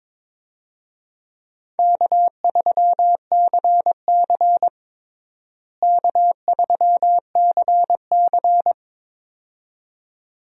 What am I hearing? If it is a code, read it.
Morse code, heard as a keyed tone, about 22 wpm, 700 Hz. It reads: K3CC K3CC
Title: USA Call Signs - 4 Characters - Repeated 2x - 22wpm